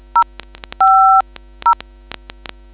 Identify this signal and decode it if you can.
DTMF, heard as telephone touch tones, tones of about 90 ms. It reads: 050